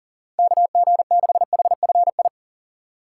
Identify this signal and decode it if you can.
Morse code, heard as a keyed tone, about 37 wpm, 700 Hz. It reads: KG6HFI
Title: USA Call Signs - 6 Characters - 2x Character Spacing - 40wpm